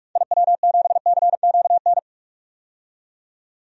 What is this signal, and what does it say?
Morse code, heard as a keyed tone, about 46 wpm, 700 Hz. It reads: IW7CQN
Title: DXCC-100 Call Signs - 6 Characters - 45wpm